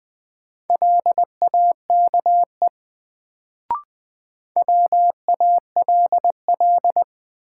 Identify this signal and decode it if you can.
Morse code, heard as a keyed tone, about 20 wpm, 700 Hz. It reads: LAKE WALL
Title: Sentences from Top 1100 Words - New Word Drill Alt - 20wpm